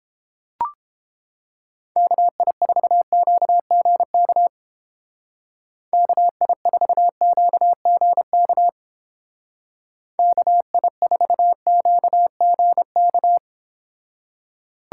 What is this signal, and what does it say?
Morse code, heard as a keyed tone, about 31 wpm, 700 Hz. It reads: KI4QGK KI4QGK KI4QGK